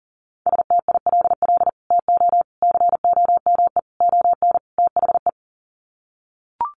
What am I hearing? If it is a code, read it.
Morse code, heard as a keyed tone, about 40 wpm, 700 Hz. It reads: STILL TO COME ON THE
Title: Sets of 5 Words Encore2 40wpm